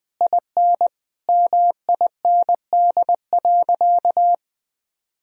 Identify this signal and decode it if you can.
Morse code, heard as a keyed tone, about 20 wpm, 700 Hz. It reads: IN MIND.
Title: Sentences from Top 1000 Words - Review Part 3 - 20wpm